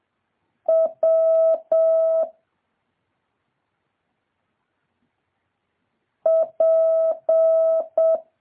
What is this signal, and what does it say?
Morse code, heard as a keyed tone, about 7 wpm, 650 Hz. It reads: W P
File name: jh8cbh / 2023cw